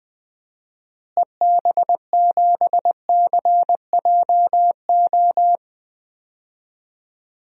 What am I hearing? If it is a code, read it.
Morse code, heard as a keyed tone, about 20 wpm, 700 Hz. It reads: EB7CJO
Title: DXCC-100 Call Signs - 6 Characters - Repeated 3x - 20wpm